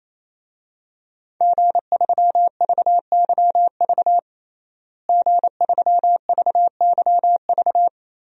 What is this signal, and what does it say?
Morse code, heard as a keyed tone, about 28 wpm, 700 Hz. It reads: G3VYV G3VYV
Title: DXCC-100 Call Signs - 5 Characters - Repeated 3x - 28wpm